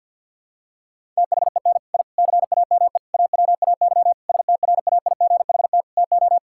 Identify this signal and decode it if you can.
Morse code, heard as a keyed tone, about 51 wpm, 700 Hz. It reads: THEN I CAME AWAY STRAIGHT TO
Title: Sentences from Top 1000 Words - Review Part 6 - 50wpm